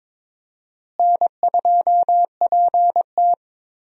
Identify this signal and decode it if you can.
Morse code, heard as a keyed tone, about 22 wpm, 700 Hz. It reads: N2PT
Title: USA Call Signs - 4 Characters - Repeated 2x - 22wpm